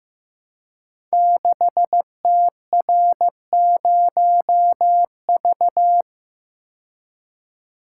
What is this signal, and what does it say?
Morse code, heard as a keyed tone, about 15 wpm, 700 Hz. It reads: 6TR0V
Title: Code Groups - 5 Letter:Numbers - 15wpm